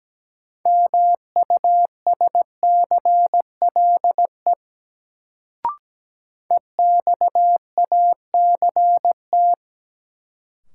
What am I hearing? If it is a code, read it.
Morse code, heard as a keyed tone, about 17 wpm, 700 Hz. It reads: MUSCLE EXACT
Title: Sentences from Top 1300 Words - New Word Drill Alt - 17wpm